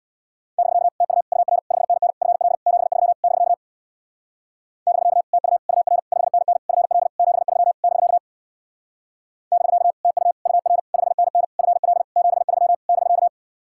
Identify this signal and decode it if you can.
Morse code, heard as a keyed tone, about 65 wpm, 700 Hz. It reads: <BK> TU UR 5NN FL 73 <BK> <BK> TU UR 5NN FL 73 <BK> <BK> TU UR 5NN FL 73 <BK>